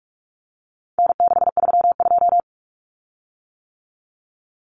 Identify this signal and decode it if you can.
Morse code, heard as a keyed tone, about 44 wpm, 700 Hz. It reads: N632
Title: Common Cut Numbers - 4-Digit 45wpm